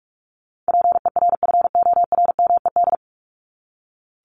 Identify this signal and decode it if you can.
Morse code, heard as a keyed tone, about 46 wpm, 700 Hz. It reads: PERFORMED
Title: ED Suffix - Top 100 Words with Base Words 45wpm